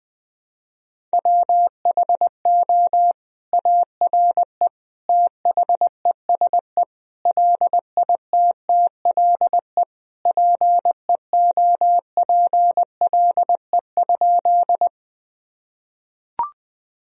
Morse code, heard as a keyed tone, about 21 wpm, 700 Hz. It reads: WHO ARE THESE LITTLE PEOPLE?
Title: Sentences from Top 1000 Words - Review Part 1 - 20wpm